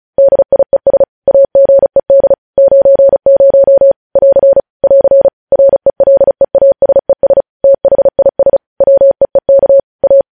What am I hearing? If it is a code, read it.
Morse code, heard as a keyed tone, about 36 wpm, 550 Hz. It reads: DIES AGED 90 <AR> <AR> RELEASES THIS WEEK A
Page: News headlines in Morse code at 35 wpm